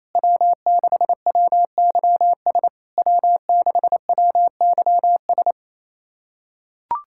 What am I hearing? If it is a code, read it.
Morse code, heard as a keyed tone, about 28 wpm, 700 Hz. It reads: W6WYS W6WYS
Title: SOTA QSO Practice 28wpm